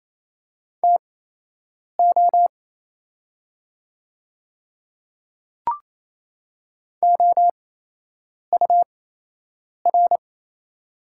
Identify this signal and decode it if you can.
Morse code, heard as a keyed tone, about 28 wpm, spaced as for 8 wpm, 700 Hz. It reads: TO OUR
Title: 8x Character Spacing - Top 200 Words - 28wpm